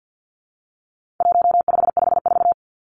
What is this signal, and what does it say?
Morse code, heard as a keyed tone, about 50 wpm, 700 Hz. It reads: 1554
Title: Numbers - 4-Digit 50wpm